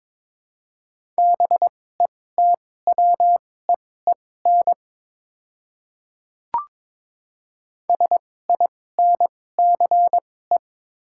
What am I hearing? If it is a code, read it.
Morse code, heard as a keyed tone, about 22 wpm, spaced as for 16 wpm, 700 Hz. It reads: BETWEEN SINCE
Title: 2x Character Spacing - Top 500 Words - 22wpm